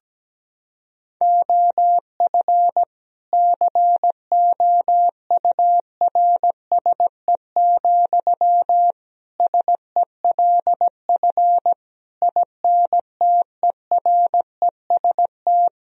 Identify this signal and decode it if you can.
Morse code, heard as a keyed tone, about 17 wpm, 700 Hz. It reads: OF COURSE, SELF INTEREST
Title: Sentences from Top 1000 Words - Review Part 2 - 17wpm